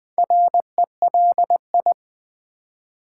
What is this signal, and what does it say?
Morse code, heard as a keyed tone, about 20 wpm, 700 Hz. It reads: RELI
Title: Build A Word - 2 Letters At A Time - 8-Letter Words - 20wpm